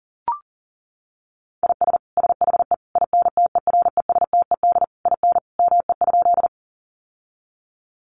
Morse code, heard as a keyed tone, about 37 wpm, 700 Hz. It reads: IS SHE INTERESTED IN ME?
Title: Sentences from Top 900 Words 40wpm